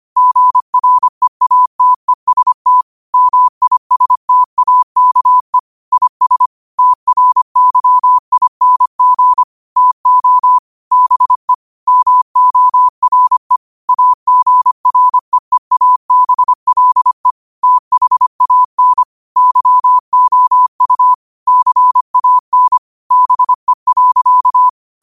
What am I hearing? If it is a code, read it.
Morse code, heard as a keyed tone, about 25 wpm, 1000 Hz. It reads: GREATEST MISTAKE IS TRYING TO BE MORE AGREEABLE THAN YOU CAN BE.
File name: qotdfiles / 25wpm